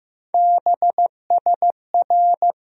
Morse code, heard as a keyed tone, about 15 wpm, 700 Hz. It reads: BSR
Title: USA Call Sign Postfixes - 3 Letters 15wpm